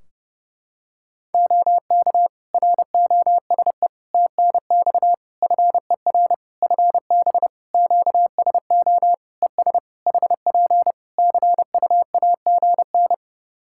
Morse code, heard as a keyed tone, about 30 wpm, 700 Hz. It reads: OK ROSE TNX FER FB QSO ES HP CUAGN
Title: Practice QSO - Exchange 4 - Ending - 30wpm